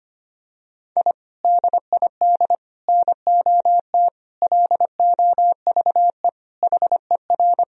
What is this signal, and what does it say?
Morse code, heard as a keyed tone, about 25 wpm, 700 Hz. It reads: I DID NOT LOVE HER
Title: Sentences from Top 1000 Words - Review Part 2 - 25wpm